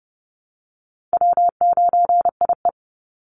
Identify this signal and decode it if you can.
Morse code, heard as a keyed tone, about 30 wpm, 700 Hz. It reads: W9IE
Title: Call Signs Repeated 2x - 30wpm